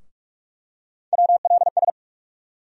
Morse code, heard as a keyed tone, about 46 wpm, 700 Hz. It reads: WLS
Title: USA Call Sign Postfixes - 3 Letters 45wpm